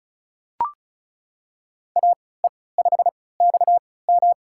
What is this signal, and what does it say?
Morse code, heard as a keyed tone, about 34 wpm, spaced as for 20 wpm, 700 Hz. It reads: AE5XM
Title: USA Call Signs - 5 Characters - 3x Character Spacing - 35wpm